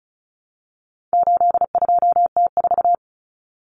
Morse code, heard as a keyed tone, about 34 wpm, 700 Hz. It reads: 82T4
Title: Common Cut Numbers - 4-Digit 35wpm